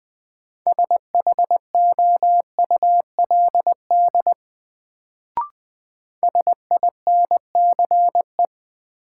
Morse code, heard as a keyed tone, about 20 wpm, 700 Hz. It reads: SHOULD SINCE
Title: Sets of 2 Words - New Word Drill - 20wpm